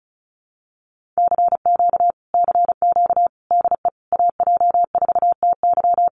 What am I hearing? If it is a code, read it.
Morse code, heard as a keyed tone, about 34 wpm, 700 Hz. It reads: CQ CQ DE AJ4TY
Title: Common QSO Phrases Encore 35wpm